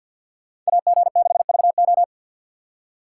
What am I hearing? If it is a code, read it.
Morse code, heard as a keyed tone, about 61 wpm, 700 Hz. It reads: AG6VO
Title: Speed Racing - USA 2x2 Call Signs 50wpm